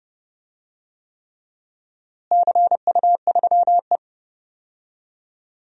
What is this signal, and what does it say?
Morse code, heard as a keyed tone, about 30 wpm, 700 Hz. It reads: CU3E
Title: DXCC-100 Call Signs - 4 Characters - Repeated 3x - 30wpm